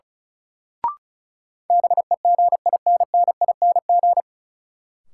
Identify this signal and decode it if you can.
Morse code, heard as a keyed tone, about 34 wpm, 700 Hz. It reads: BEGINNING
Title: Sentences from Top 800 Words - New Word Drill - 35wpm